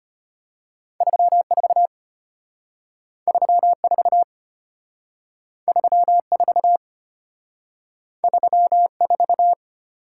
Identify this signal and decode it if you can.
Morse code, heard as a keyed tone, about 32 wpm, 700 Hz. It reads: 34 34 34 34